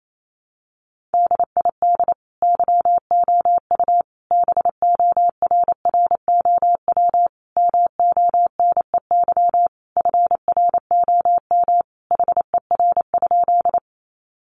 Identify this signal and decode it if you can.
Morse code, heard as a keyed tone, about 28 wpm, 700 Hz. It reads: DID YOU BORROW MONEY FROM HER?
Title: Sentences from Top 1500 Words 28wpm